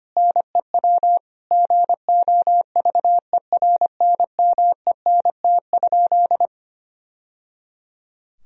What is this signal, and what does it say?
Morse code, heard as a keyed tone, about 25 wpm, 700 Hz. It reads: NEW GOVERNMENT?
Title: Sentences from Top 1000 Words - Review Part 3 - 25wpm